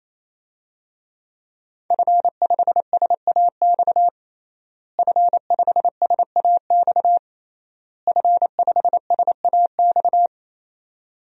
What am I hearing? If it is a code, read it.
Morse code, heard as a keyed tone, about 28 wpm, 700 Hz. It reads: F5SAX F5SAX F5SAX
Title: DXCC-100 Call Signs - 5 Characters - Repeated 3x - 28wpm